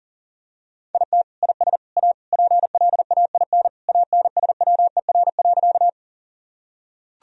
Morse code, heard as a keyed tone, about 40 wpm, 700 Hz. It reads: IT IS A PLAIN ANSWER.
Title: Sentences from Top 1000 Words - Review Part 2 - 40wpm